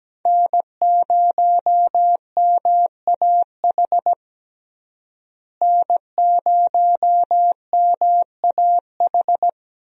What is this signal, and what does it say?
Morse code, heard as a keyed tone, about 17 wpm, 700 Hz. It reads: N0MAH N0MAH
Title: USA Call Signs - 5 Characters - Repeated 3x - 17wpm